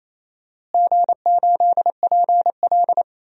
Morse code, heard as a keyed tone, about 28 wpm, 700 Hz. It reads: G8PL
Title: DXCC-100 Call Signs - 4 Characters - Repeated 3x - 28wpm